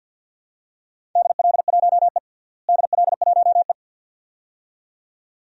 Morse code, heard as a keyed tone, about 51 wpm, 700 Hz. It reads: DL1E DL1E
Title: DXCC-100 Call Signs - 4 Characters - Repeated 2x - 50wpm